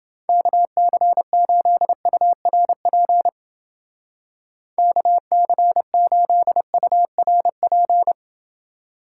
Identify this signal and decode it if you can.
Morse code, heard as a keyed tone, about 30 wpm, 700 Hz. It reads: KC8URP KC8URP